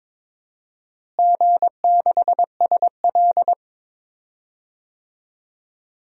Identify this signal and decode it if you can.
Morse code, heard as a keyed tone, about 22 wpm, 700 Hz. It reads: G6SL
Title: DXCC-100 Call Signs - 4 Characters - Repeated 2x - 22wpm